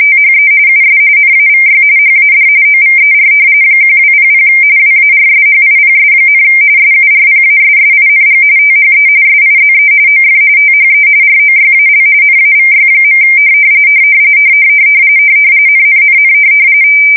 RTTY, about 45.45 baud. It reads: THE US NAVY HAD OVER 1200 WARSHIPS, SURPASSING THE SIZE OF THE ROYAL NAVY. NNNN
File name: fsk170 / usnavy4